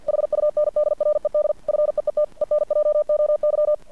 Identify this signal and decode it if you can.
Morse code, heard as a keyed tone, about 50 wpm, 600 Hz. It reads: HUNDRED FEET ENJOY
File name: archived / CRSnet